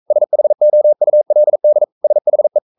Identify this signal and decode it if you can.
Morse code, heard as a keyed tone, about 44 wpm, 600 Hz. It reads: SHOULD SHE
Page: Romanen Never af Ken Follet på engelsk.